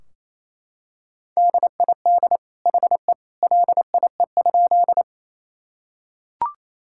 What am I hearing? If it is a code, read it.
Morse code, heard as a keyed tone, about 28 wpm, 700 Hz. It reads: DID HE LIE?
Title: Sentences from Top 1600 Words 28wpm